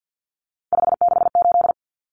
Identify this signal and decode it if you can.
Morse code, heard as a keyed tone, about 49 wpm, 700 Hz. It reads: 568
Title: Numbers - 3-Digit 50wpm